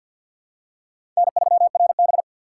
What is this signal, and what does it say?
Morse code, heard as a keyed tone, about 52 wpm, 700 Hz. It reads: N3RB
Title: USA Call Signs - 4 Characters - 2x Character Spacing - 50wpm